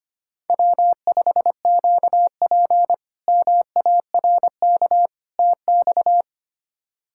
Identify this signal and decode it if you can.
Morse code, heard as a keyed tone, about 25 wpm, 700 Hz. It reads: W5QP MARK TX
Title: K1USN SST QSO Practice - 25wpm